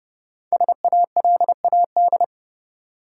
Morse code, heard as a keyed tone, about 30 wpm, 700 Hz. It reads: SALAD
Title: Five Letter Words - 30wpm